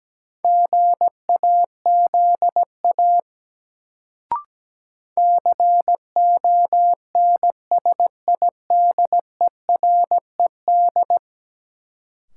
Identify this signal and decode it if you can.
Morse code, heard as a keyed tone, about 17 wpm, 700 Hz. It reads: GAZA CONSIDERED
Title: Sets of 5 Words - New Word Drill - 17wpm